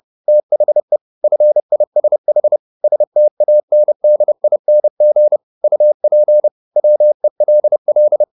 Morse code, heard as a keyed tone, about 30 wpm, 600 Hz. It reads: THE FISH STANDING UP WELL